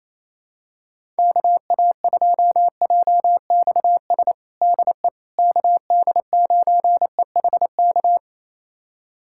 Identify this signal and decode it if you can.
Morse code, heard as a keyed tone, about 28 wpm, 700 Hz. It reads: KA2JXS DE KD9EHK
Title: USA Callsign DE Callsign - 6 Characters - 28wpm